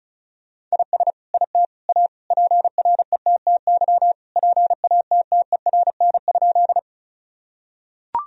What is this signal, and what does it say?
Morse code, heard as a keyed tone, about 36 wpm, 700 Hz. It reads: IS IT A PRETTY PATTERN?
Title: Sentences from Top 1000 Words - Review Part 2 - 35wpm